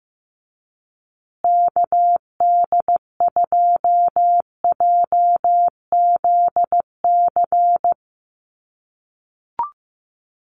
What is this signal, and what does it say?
Morse code, heard as a keyed tone, about 15 wpm, 700 Hz. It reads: KD2JZC
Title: USA Call Signs Repeated 2x 15wpm